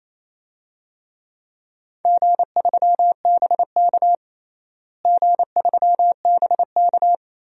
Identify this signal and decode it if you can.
Morse code, heard as a keyed tone, about 28 wpm, 700 Hz. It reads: G3BK G3BK
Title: DXCC-100 Call Signs - 4 Characters - Repeated 3x - 28wpm